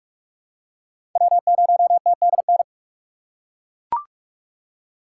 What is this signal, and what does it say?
Morse code, heard as a keyed tone, about 41 wpm, 700 Hz. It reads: W0TDN
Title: USA Call Signs - 5 Characters - 3x Character Spacing - 45wpm